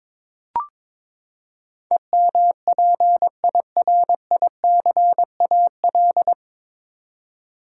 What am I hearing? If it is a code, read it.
Morse code, heard as a keyed tone, about 21 wpm, 700 Hz. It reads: EMPIRICAL
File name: practice / 22wpm / Nine Letter Words - 22wpm